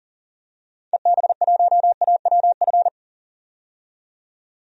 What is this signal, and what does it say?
Morse code, heard as a keyed tone, about 40 wpm, 700 Hz. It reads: EB1AWF
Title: DXCC-100 Call Signs - 6 Characters - 40wpm